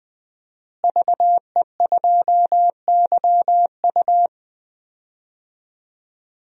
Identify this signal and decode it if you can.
Morse code, heard as a keyed tone, about 21 wpm, 700 Hz. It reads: VE2YU
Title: DXCC-100 Call Signs - 5 Characters - Repeated 2x - 20wpm